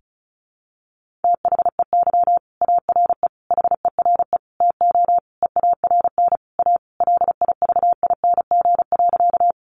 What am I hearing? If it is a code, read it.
Morse code, heard as a keyed tone, about 34 wpm, 700 Hz. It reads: THEY ARE HERE TO EARN A LIVING.
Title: Sentences from Top 1500 Words 35wpm